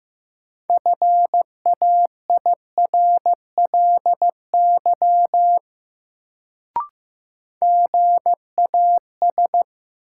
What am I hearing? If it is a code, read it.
Morse code, heard as a keyed tone, about 15 wpm, 700 Hz. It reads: FAIRLY GAS
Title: Sentences from Top 1100 Words - New Word Drill Alt - 15wpm